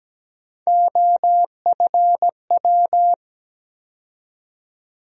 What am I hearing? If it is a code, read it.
Morse code, heard as a keyed tone, about 17 wpm, 700 Hz. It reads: OFW
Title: USA Call Sign Postfixes - 3 Letters 17wpm